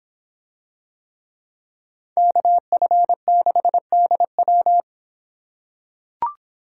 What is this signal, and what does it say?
Morse code, heard as a keyed tone, about 25 wpm, 700 Hz. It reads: KF6DW